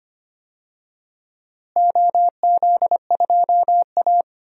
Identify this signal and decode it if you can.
Morse code, heard as a keyed tone, about 25 wpm, 700 Hz. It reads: OZ2A
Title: DXCC-100 Call Signs - 4 Characters - Repeated 3x - 25wpm